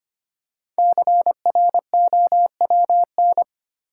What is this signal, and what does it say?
Morse code, heard as a keyed tone, about 25 wpm, 690 Hz. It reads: CROWN
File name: practice / 25wpm / Easy Words with Hints - 25wpm